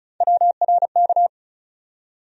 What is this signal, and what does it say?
Morse code, heard as a keyed tone, about 34 wpm, 700 Hz. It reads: WRK
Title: USA Call Sign Postfixes - 3 Letters 35wpm